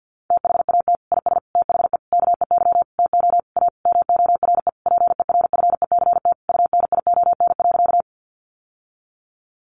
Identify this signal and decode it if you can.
Morse code, heard as a keyed tone, about 50 wpm, 700 Hz. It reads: THAT IS THE KEY TO A MORE PERFECT UNION.
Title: Sentences from Top 1400 Words 50wpm